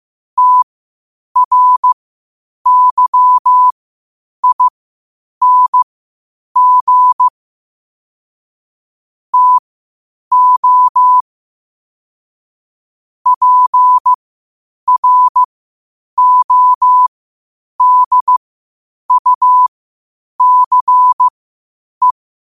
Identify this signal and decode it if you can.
Morse code, heard as a keyed tone, about 15 wpm, spaced as for 8 wpm, 1000 Hz. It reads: TRYING TO PRODUCE
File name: qotdfiles / 8wpm